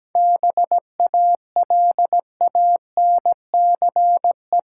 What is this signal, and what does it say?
Morse code, heard as a keyed tone, about 17 wpm, 700 Hz. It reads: BALANCE
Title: Sentences from Top 900 Words - New Word Drill - 17wpm